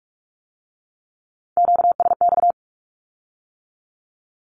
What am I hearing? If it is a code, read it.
Morse code, heard as a keyed tone, about 46 wpm, 700 Hz. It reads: QSX
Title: Q-Codes with Spoken Meaning - 45wpm